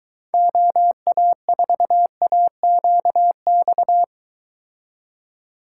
Morse code, heard as a keyed tone, about 23 wpm, 700 Hz. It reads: OA4AQX